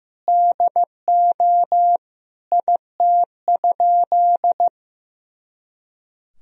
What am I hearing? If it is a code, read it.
Morse code, heard as a keyed tone, about 15 wpm, 700 Hz. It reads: DO IT?
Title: Sentences from Top 100 Words and Abbreviations 15wpm